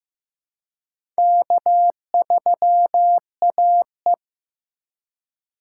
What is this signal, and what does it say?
Morse code, heard as a keyed tone, about 15 wpm, 700 Hz. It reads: K3AE
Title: USA Call Signs - 4 Characters - 2x Character Spacing - 15wpm